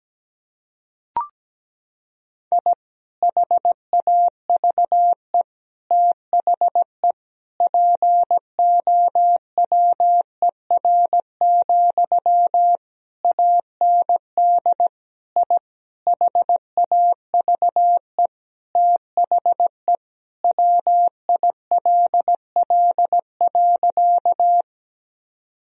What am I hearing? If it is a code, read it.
Morse code, heard as a keyed tone, about 17 wpm, 700 Hz. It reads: I HAVE THE POWER, AND I HAVE THE WILL.
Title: Sentences from Top 1000 Words - Review Part 2 - 17wpm